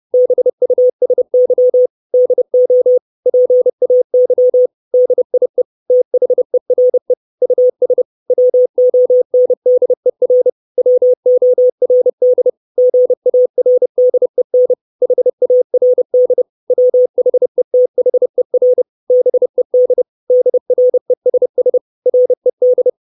Morse code, heard as a keyed tone, about 30 wpm, 500 Hz. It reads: BUSY DO PAY DIE THERE US WONDER WORD GARDEN HARD WHETHER BED DRESS RED
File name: 632 Words 30WPM